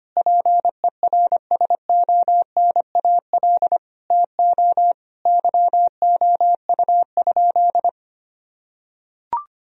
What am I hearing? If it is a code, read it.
Morse code, heard as a keyed tone, about 25 wpm, 700 Hz. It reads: PERSONAL TO YOU?
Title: Sentences from Top 1000 Words - Review Part 4 - 25wpm